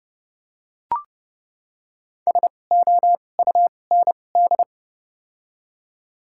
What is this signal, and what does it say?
Morse code, heard as a keyed tone, about 30 wpm, spaced as for 22 wpm, 700 Hz. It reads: SOUND
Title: 2x Character Spacing - Top 500 Words - 30wpm